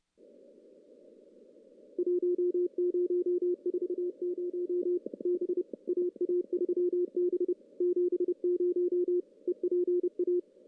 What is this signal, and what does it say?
Morse code, heard as a keyed tone, about 30 wpm, 360 Hz. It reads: 1040 B AA3B 70 EPA
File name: contests / CW-SS-k5tr- -1600